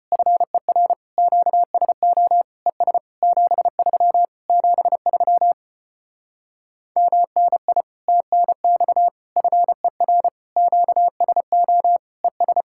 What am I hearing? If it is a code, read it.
Morse code, heard as a keyed tone, about 33 wpm, 700 Hz. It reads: FER QSO ES 73 73 MNI TNX FER QSO ES